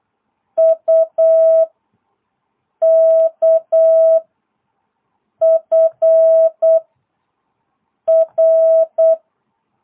Morse code, heard as a keyed tone, about 8 wpm, 660 Hz. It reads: U K F R